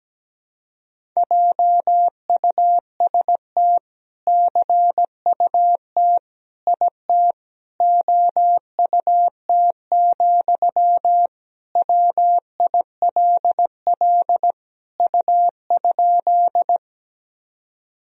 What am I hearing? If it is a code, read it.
Morse code, heard as a keyed tone, about 17 wpm, 700 Hz. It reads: JUST CUT IT OUT, WILL U?
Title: Sentences from Top 300 Words and Abbreviations 17wpm